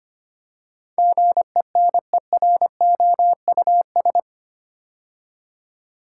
Morse code, heard as a keyed tone, about 25 wpm, 700 Hz. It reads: GENEROUS
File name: 8 Letter Words - First 3 Letters Given - 25wpm